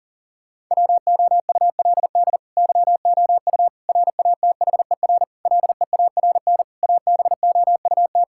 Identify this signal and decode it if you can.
Morse code, heard as a keyed tone, about 37 wpm, 700 Hz. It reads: WOULD YOU RATHER LEARN ABOUT
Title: Sentences from Top 1000 Words - Review Part 6 - 40wpm